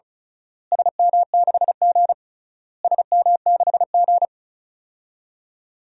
Morse code, heard as a keyed tone, about 36 wpm, 700 Hz. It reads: SM6G SM6G
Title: DXCC-100 Call Signs - 4 Characters - Repeated 3x - 35wpm